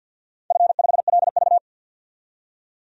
Morse code, heard as a keyed tone, about 52 wpm, 700 Hz. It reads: F5LV
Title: DXCC-100 Call Signs - 4 Characters - 50wpm